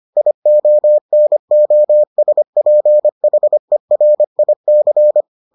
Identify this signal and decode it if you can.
Morse code, heard as a keyed tone, about 25 wpm, 600 Hz. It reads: IONOSPHERIC